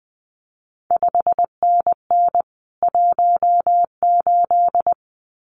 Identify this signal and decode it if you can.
Morse code, heard as a keyed tone, about 20 wpm, 700 Hz. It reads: 5NN 18
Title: CQ WW DX Contest Exchange Encore - 20wpm